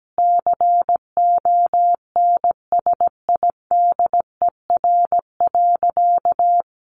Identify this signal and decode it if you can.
Morse code, heard as a keyed tone, about 17 wpm, 700 Hz. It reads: CONSIDER.
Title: Sentences from Top 900 Words 17wpm